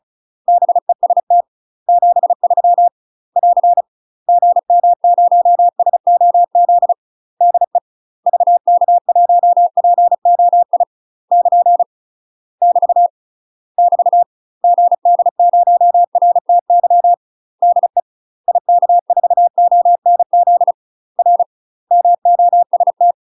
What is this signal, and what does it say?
Morse code, heard as a keyed tone, about 35 wpm, 700 Hz. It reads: BEST 73 <AR> GM0SOZ DE VK1POI <KN> <BT> <BT> GD0RTY DE IK4ONZ R MOST
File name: UK MASTER QSOs TRAINING / 35WPM / UK MASTER QSOs 0040